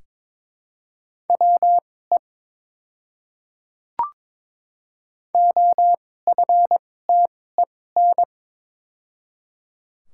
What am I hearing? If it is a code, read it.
Morse code, heard as a keyed tone, about 22 wpm, spaced as for 16 wpm, 700 Hz. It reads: WE OFTEN
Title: 2x Character Spacing - Top 300 Words - 22wpm